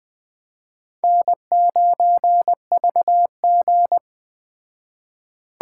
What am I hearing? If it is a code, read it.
Morse code, heard as a keyed tone, about 21 wpm, 700 Hz. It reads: N9VG